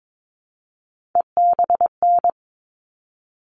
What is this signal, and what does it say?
Morse code, heard as a keyed tone, about 22 wpm, 700 Hz. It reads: EBN
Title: All Cut Numbers - 3-Digit 22wpm